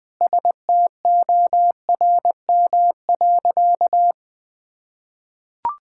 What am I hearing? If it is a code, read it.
Morse code, heard as a keyed tone, about 21 wpm, 700 Hz. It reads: STORM.
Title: Sentences with 4-Word Phrases - 20wpm